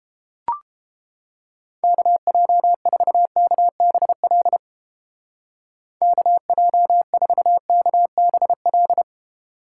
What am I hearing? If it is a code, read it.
Morse code, heard as a keyed tone, about 30 wpm, 700 Hz. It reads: KJ4KBL KJ4KBL